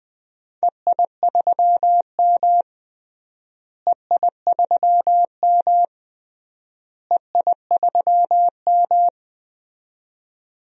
Morse code, heard as a keyed tone, about 20 wpm, 700 Hz. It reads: EI3M EI3M EI3M
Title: DXCC-100 Call Signs - 4 Characters - Repeated 3x - 20wpm